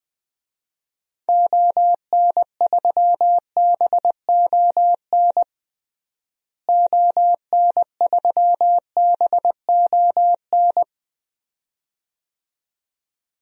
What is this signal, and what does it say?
Morse code, heard as a keyed tone, about 20 wpm, 700 Hz. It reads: ON3BON ON3BON
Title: DXCC-100 Call Signs - 6 Characters - Repeated 2x - 20wpm